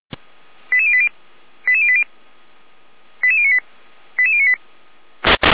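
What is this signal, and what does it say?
Five-tone selective call (ZVEI), heard as quick runs of five tones, tones of about 70 ms. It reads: 88080 88080 88098 88098
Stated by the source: ZVEI five-tone call received by SDR.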